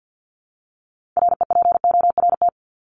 Morse code, heard as a keyed tone, about 50 wpm, 700 Hz. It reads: REPORT
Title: Top 751-1000 Words 50wpm